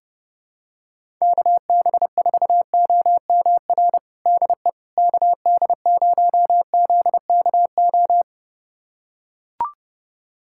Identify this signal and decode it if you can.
Morse code, heard as a keyed tone, about 30 wpm, 700 Hz. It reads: KB4OMR DE KD0ZKO
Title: USA Callsign DE Callsign - 6 Characters - 30wpm